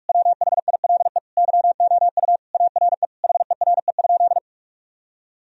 Morse code, heard as a keyed tone, about 48 wpm, 700 Hz. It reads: WHILE YOU ARE HERE?
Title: Sentences from Top 1000 Words - Review Part 1 - 45wpm